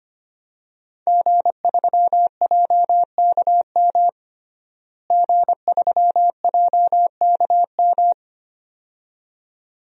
Morse code, heard as a keyed tone, about 25 wpm, 700 Hz. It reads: G3JKM G3JKM
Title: DXCC-100 Call Signs - 5 Characters - Repeated 2x - 25wpm